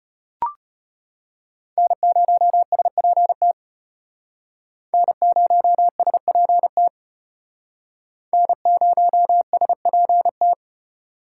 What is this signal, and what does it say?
Morse code, heard as a keyed tone, about 34 wpm, 700 Hz. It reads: N0SPT N0SPT N0SPT